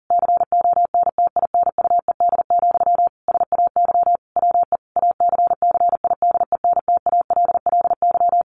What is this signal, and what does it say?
Morse code, heard as a keyed tone, about 40 wpm, 700 Hz. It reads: CONTINUED, SAY WE ACCIDENTALLY
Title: A Sound Of Thunder 40wpm